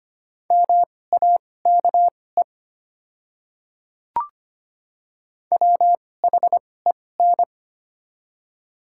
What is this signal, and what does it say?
Morse code, heard as a keyed tone, about 25 wpm, spaced as for 18 wpm, 690 Hz. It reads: MAKE WHEN
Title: 2x Character Spacing - Top 100 Words - 25wpm